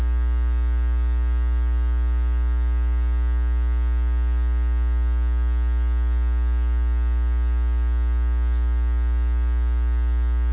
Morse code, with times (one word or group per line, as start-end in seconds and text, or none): none